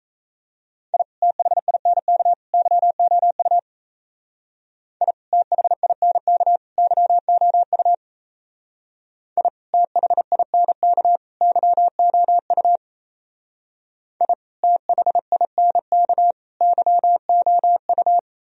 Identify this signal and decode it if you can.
Morse code, heard as a keyed tone, about 36 wpm, 700 Hz. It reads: I THINK YOU I THINK YOU I THINK YOU I THINK YOU